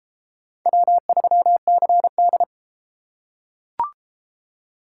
Morse code, heard as a keyed tone, about 32 wpm, 700 Hz. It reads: W3CD